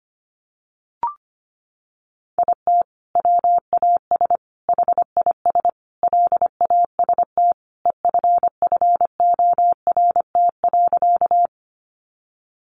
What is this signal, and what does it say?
Morse code, heard as a keyed tone, about 25 wpm, 700 Hz. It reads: IT WAS HIS LAST EFFORT.
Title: Sentences from Top 1200 Words 25wpm